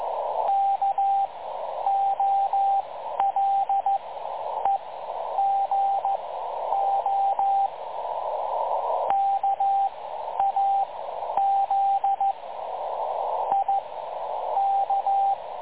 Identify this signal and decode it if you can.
Morse code, heard as a keyed tone, about 14 wpm, spaced as for 9 wpm, 780 Hz, with radio noise. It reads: KOLEGO KAZIK
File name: dlazaawansowanych / nagraniazpasma / websdr recording 3565.0kHz